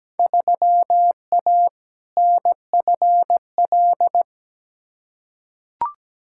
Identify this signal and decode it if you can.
Morse code, heard as a keyed tone, about 17 wpm, 700 Hz. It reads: 3A NFL
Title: Field Day Exchanges - 17wpm